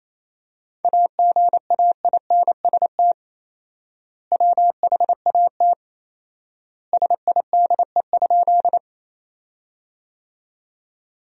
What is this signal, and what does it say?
Morse code, heard as a keyed tone, about 27 wpm, 700 Hz. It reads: AGAINST WHAT SIDE?
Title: 4x Word-Spacing - Sentences from Top 400 Words - 28wpm